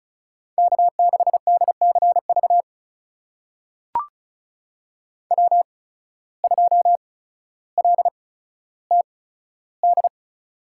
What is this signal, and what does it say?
Morse code, heard as a keyed tone, about 36 wpm, spaced as for 15 wpm, 700 Hz. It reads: K6DCV W2LTD
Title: USA Call Signs - 5 Characters - 8x Character Spacing - 35wpm